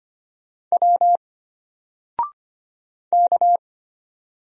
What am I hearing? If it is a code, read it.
Morse code, heard as a keyed tone, about 25 wpm, 700 Hz. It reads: W K
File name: Single-Letter-Number-25wpm